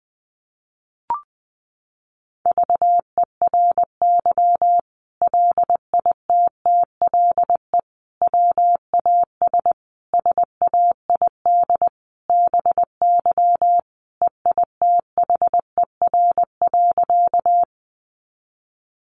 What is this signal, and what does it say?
Morse code, heard as a keyed tone, about 20 wpm, 700 Hz. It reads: VERY LITTLE WAS SAID BY EITHER.
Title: Sentences from Top 600 Words 20wpm